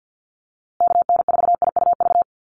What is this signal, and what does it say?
Morse code, heard as a keyed tone, about 50 wpm, 700 Hz. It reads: KN4IUV
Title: USA Call Signs Repeated 3x 50wpm